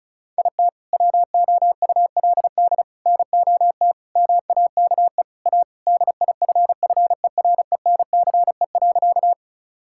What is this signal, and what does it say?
Morse code, heard as a keyed tone, about 36 wpm, 700 Hz. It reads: IT WOULD NOT MAKE A DIFFERENCE.
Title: Sentences from Top 1000 Words - Review Part 5 - 35wpm